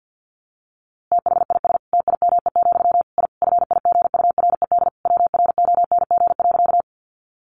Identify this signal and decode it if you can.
Morse code, heard as a keyed tone, about 50 wpm, 700 Hz. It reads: THIS TIME, I FIGURED WRONG.
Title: A Sound Of Thunder 50wpm